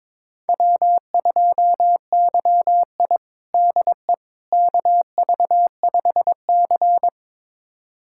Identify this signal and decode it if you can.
Morse code, heard as a keyed tone, about 22 wpm, 700 Hz. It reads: W2YI DE KV5C
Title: USA Callsign DE Callsign - 4 Characters - 22wpm